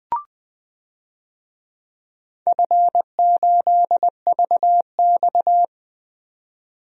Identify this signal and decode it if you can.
Morse code, heard as a keyed tone, about 20 wpm, 700 Hz. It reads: F8VX
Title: DXCC-100 Call Signs - 4 Characters - 20wpm